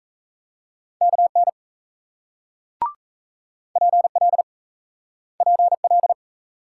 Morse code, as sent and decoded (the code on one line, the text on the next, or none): -.- -. / .--. .-.. / .--. .-..
KN PL PL